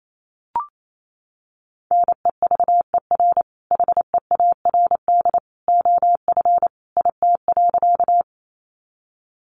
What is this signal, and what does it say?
Morse code, heard as a keyed tone, about 28 wpm, 700 Hz. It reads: NEVER HEARD OF IT.
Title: Sentences from Top 400 Words 28wpm